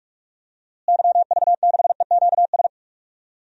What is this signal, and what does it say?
Morse code, heard as a keyed tone, about 46 wpm, 700 Hz. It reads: YV6EQS
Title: DXCC-100 Call Signs - 6 Characters - Repeated 3x - 45wpm